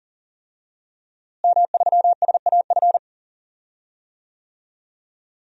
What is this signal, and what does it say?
Morse code, heard as a keyed tone, about 41 wpm, 700 Hz. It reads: M3SAF
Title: DXCC-100 Call Signs - 5 Characters - Repeated 3x - 40wpm